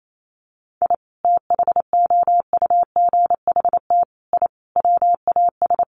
Morse code, heard as a keyed tone, about 28 wpm, 700 Hz. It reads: I THOUGHT I WAS
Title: Sets of 5 Words Encore2 28wpm